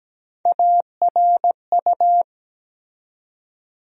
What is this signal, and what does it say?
Morse code, heard as a keyed tone, about 17 wpm, 700 Hz. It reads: ARU
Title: USA Call Sign Postfixes - 3 Letters 17wpm